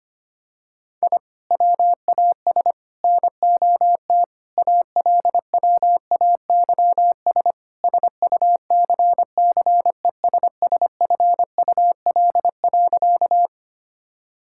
Morse code, heard as a keyed tone, about 25 wpm, 700 Hz. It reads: I WAS NOT ALWAYS SUCCESSFUL.
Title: Sentences from Top 1000 Words - Review Part 6 - 25wpm